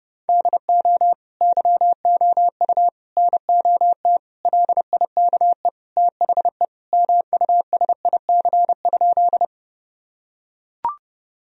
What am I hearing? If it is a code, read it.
Morse code, heard as a keyed tone, about 32 wpm, 700 Hz. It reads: DO YOU NOT LIKE THE MUSIC?
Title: Sentences from Top 1000 Words - Review Part 1 - 30wpm